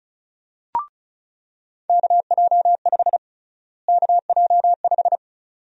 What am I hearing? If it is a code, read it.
Morse code, heard as a keyed tone, about 36 wpm, 700 Hz. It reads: KJ5 KJ5
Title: USA Call Sign Prefixes - Repeated 2x - 35wpm